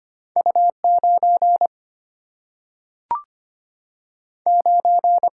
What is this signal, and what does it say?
Morse code, heard as a keyed tone, about 25 wpm, 700 Hz. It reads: U9 9
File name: Two Letter-Number Combinations - 25wpm